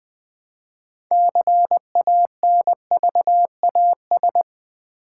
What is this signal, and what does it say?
Morse code, heard as a keyed tone, about 20 wpm, 700 Hz. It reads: CANVAS
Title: 6 Letter Words - First 3 Letters Given - 20wpm